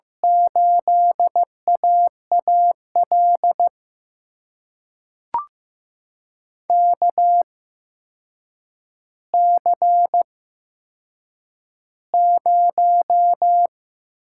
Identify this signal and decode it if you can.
Morse code, heard as a keyed tone, about 15 wpm, spaced as for 7 wpm, 700 Hz. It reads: 8AAL KC0
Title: USA Call Signs - 6 Characters - 8x Character Spacing - 15wpm